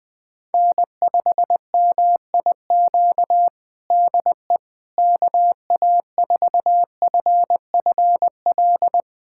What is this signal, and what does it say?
Morse code, heard as a keyed tone, about 20 wpm, 700 Hz. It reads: N5MIQ DE KA4FFL
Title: USA Callsign DE Callsign 20wpm